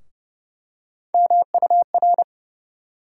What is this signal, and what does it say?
Morse code, heard as a keyed tone, about 30 wpm, 700 Hz. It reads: MUR
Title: USA Call Sign Postfixes - 3 Letters 30wpm